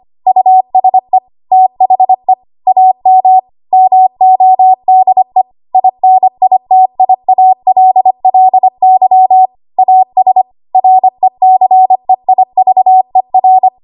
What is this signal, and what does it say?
Morse code, heard as a keyed tone, about 25 wpm, 750 Hz. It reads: USE THE AM MODE INITIALLY AS RECEIVER
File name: Archive / 25 WPM / 25WPM